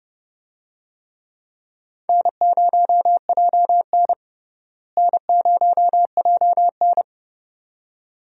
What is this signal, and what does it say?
Morse code, heard as a keyed tone, about 32 wpm, 700 Hz. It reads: N0JN N0JN
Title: USA Call Signs - 4 Characters - Repeated 2x - 30wpm